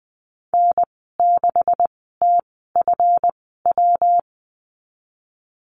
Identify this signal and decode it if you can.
Morse code, heard as a keyed tone, about 20 wpm, spaced as for 15 wpm, 700 Hz. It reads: N6TFW
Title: 167 Lesson 27g - Number 7 - Call Signs 20wpm